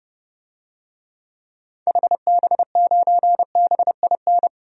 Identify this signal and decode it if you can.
Morse code, heard as a keyed tone, about 30 wpm, 700 Hz. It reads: HB9BIN
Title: DXCC-100 Call Signs - 6 Characters - 30wpm